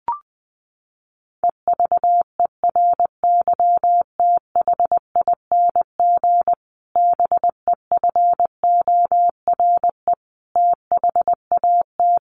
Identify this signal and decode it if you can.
Morse code, heard as a keyed tone, about 20 wpm, 700 Hz. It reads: EVERYTHING BEFORE THAT
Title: Sentences from Top 1500 Words 20wpm